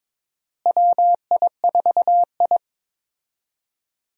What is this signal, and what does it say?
Morse code, heard as a keyed tone, about 21 wpm, 700 Hz. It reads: WI4I
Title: USA Call Signs - 4 Characters - Repeated 3x - 22wpm